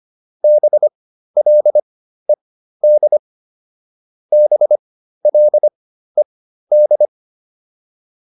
Morse code, heard as a keyed tone, about 25 wpm, spaced as for 13 wpm, 600 Hz. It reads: BLED BLED